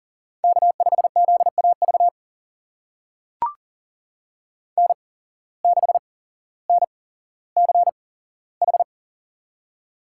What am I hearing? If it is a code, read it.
Morse code, heard as a keyed tone, about 40 wpm, spaced as for 17 wpm, 700 Hz. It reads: K5ZAV N6NCH
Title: USA Call Signs - 5 Characters - 8x Character Spacing - 40wpm